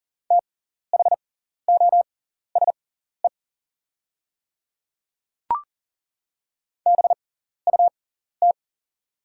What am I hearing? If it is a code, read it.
Morse code, heard as a keyed tone, about 40 wpm, spaced as for 14 wpm, 690 Hz. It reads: THOSE BUT
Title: 6x Character Spacing - Top 500 Words - 40wpm